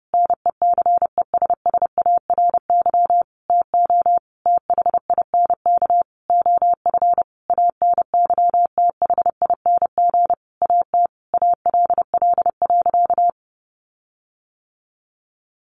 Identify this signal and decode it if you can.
Morse code, heard as a keyed tone, about 30 wpm, 700 Hz. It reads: NECESSARY TO THINK OF ANYTHING AT ALL.
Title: Sentences from Top 1300 Words 30wpm